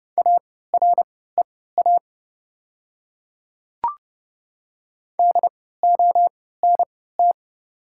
Morse code, heard as a keyed tone, about 32 wpm, spaced as for 17 wpm, 690 Hz. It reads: AREA DONT
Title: 3x Character Spacing - Top 500 Words - 30wpm